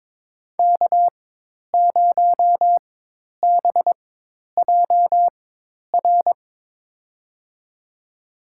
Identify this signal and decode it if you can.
Morse code, heard as a keyed tone, about 22 wpm, spaced as for 10 wpm, 700 Hz. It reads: K0BJR
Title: USA Call Signs - 5 Characters - 4x Character Spacing - 22wpm